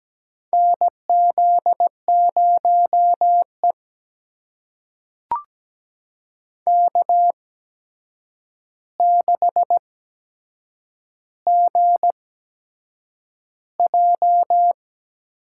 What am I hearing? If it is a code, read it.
Morse code, heard as a keyed tone, about 17 wpm, spaced as for 7 wpm, 700 Hz. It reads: NZ0E K6GJ
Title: USA Call Signs - 4 Characters - 8x Character Spacing - 17wpm